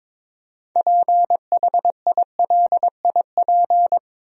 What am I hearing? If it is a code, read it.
Morse code, heard as a keyed tone, about 22 wpm, 700 Hz. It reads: PHILIP
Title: USA Common Names - 22wpm